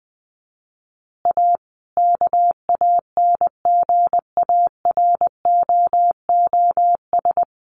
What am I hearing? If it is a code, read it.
Morse code, heard as a keyed tone, about 20 wpm, 700 Hz. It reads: A KANGAROOS
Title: Easy Phrases 20wpm